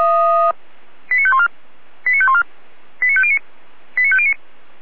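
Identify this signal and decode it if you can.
Five-tone selective call (ZVEI), heard as quick runs of five tones, tones of about 70 ms. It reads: 89514 89514 89509 89509
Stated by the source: ZVEI five-tone call received on SDR.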